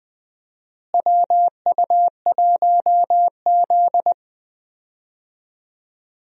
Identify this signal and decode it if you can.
Morse code, heard as a keyed tone, about 20 wpm, 700 Hz. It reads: WU1Z
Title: USA Call Signs - 4 Characters - Repeated 3x - 20wpm